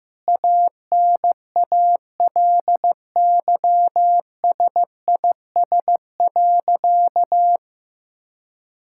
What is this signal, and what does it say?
Morse code, heard as a keyed tone, about 15 wpm, 700 Hz. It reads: ANALYSIS.
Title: Sentences from Top 1000 Words - Review Part 5 - 15wpm